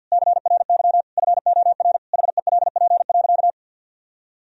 Morse code, heard as a keyed tone, about 51 wpm, 700 Hz. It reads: CRY FOR HELP.
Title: Sentences from Top 1000 Words - Review Part 3 - 50wpm